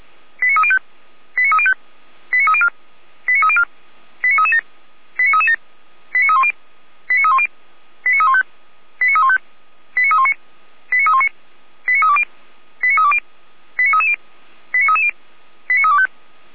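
Five-tone selective call (ZVEI), heard as quick runs of five tones, tones of about 70 ms. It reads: 89305 89305 89304 89304 89337 89337 89310 89310 89315 89315 89319 89319 89320 89320 89330 89330 89325
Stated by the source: ZVEI five-tone call received by SDR, archived by feuerwehr-orlamuende.org fire brigade server.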